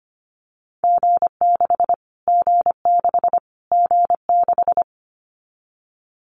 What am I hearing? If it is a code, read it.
Morse code, heard as a keyed tone, about 25 wpm, 700 Hz. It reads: G6 G6 G6
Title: Callsign Prefixes for Top 25 DXCC - Repeated 3x - 25wpm